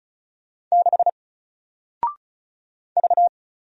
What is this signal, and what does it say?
Morse code, heard as a keyed tone, about 36 wpm, 700 Hz. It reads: 6 V
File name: Single-Letter-Number-35wpm